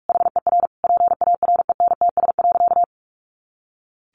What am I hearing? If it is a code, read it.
Morse code, heard as a keyed tone, about 44 wpm, 700 Hz. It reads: HER PARENTS.
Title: Easy Phrases 45wpm